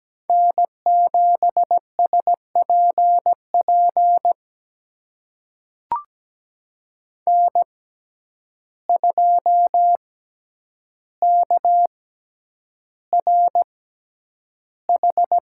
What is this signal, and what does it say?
Morse code, heard as a keyed tone, about 17 wpm, spaced as for 9 wpm, 700 Hz. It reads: N7SPP N2KRH
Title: USA Call Signs - 6x Character Spacing - 17wpm